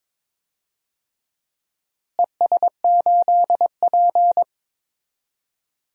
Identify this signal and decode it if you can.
Morse code, heard as a keyed tone, about 22 wpm, 700 Hz. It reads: ES8P
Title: DXCC-100 Call Signs - 4 Characters - Repeated 3x - 22wpm